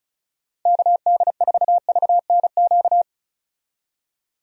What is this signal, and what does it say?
Morse code, heard as a keyed tone, about 36 wpm, 700 Hz. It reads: KD4VNQ
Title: USA Call Signs - 6 Characters - 35wpm